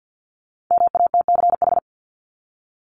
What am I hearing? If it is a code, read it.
Morse code, heard as a keyed tone, about 50 wpm, 700 Hz. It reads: MATCH
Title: Top 1501-1750 Words 50wpm